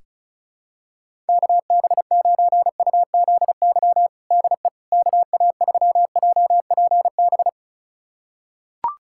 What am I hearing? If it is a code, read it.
Morse code, heard as a keyed tone, about 34 wpm, 700 Hz. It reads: KB9UZY DE KA3JPB
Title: USA Callsign DE Callsign - 6 Characters - 35wpm